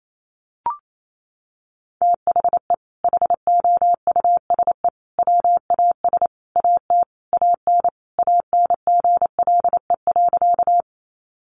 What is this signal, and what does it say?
Morse code, heard as a keyed tone, about 28 wpm, 700 Hz. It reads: THE HOUSE WAS AT AN ANGLE.
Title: Sentences from Top 1500 Words 28wpm